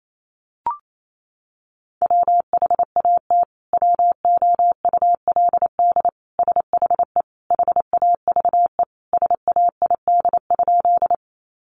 Morse code, heard as a keyed tone, about 28 wpm, 700 Hz. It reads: WHAT WOULD SHE HAVE SAID?
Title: Sentences from Top 100 Words 28wpm